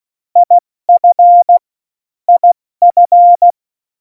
Morse code, heard as a keyed tone, about 16 wpm, 700 Hz. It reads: IF IF
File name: Morsekurs / Witze / Witz 25 Juli 17, Tempo 70 F80